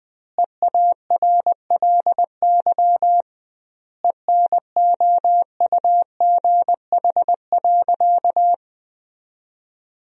Morse code, heard as a keyed tone, about 20 wpm, 700 Hz. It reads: EARLY ENOUGH.
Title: 2x Word-Spacing - Sentences from Top 400 Words - 20wpm